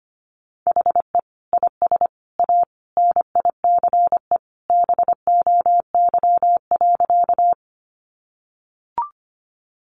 Morse code, heard as a keyed tone, about 25 wpm, 700 Hz. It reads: HE IS A NICE BOY.
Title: Sentences from Top 1100 Words 25wpm